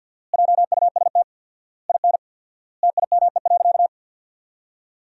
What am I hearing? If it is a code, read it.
Morse code, heard as a keyed tone, about 50 wpm, 700 Hz. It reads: JUST IN TIME.
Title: 4x Word-Spacing - Sentences from Top 300 Words - 50wpm